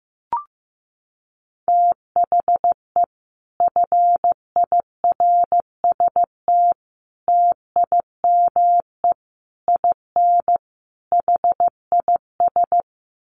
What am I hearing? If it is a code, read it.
Morse code, heard as a keyed tone, about 15 wpm, 700 Hz. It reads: THE FIRST TIME IN HIS
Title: Sets of 5 Words Encore2 15wpm